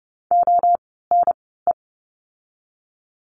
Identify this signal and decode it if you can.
Morse code, heard as a keyed tone, about 30 wpm, spaced as for 17 wpm, 700 Hz. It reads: ONE
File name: course / 30:10wpm / 230 Lesson 37f - Letter x - Words 30wpm